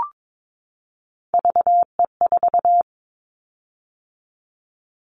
Morse code, heard as a keyed tone, about 22 wpm, 700 Hz. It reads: VE4
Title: Callsign Prefixes for Top 50 DXCC - 22wpm